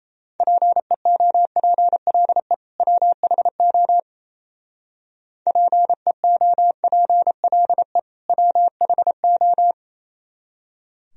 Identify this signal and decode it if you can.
Morse code, heard as a keyed tone, about 30 wpm, 700 Hz. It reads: PEOPLE WHO PEOPLE WHO
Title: Speed-Racing - Sets of 2 Words 28wpm